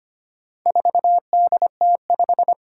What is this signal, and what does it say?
Morse code, heard as a keyed tone, about 25 wpm, 700 Hz. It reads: 4DT5
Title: Four Challenging Letter-Number Combinations 25wpm